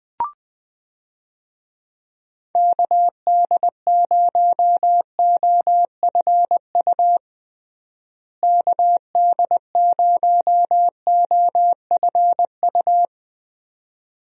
Morse code, heard as a keyed tone, about 21 wpm, 700 Hz. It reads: KD0OFU KD0OFU
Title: USA Call Signs - 6 Characters - Repeated 2x - 20wpm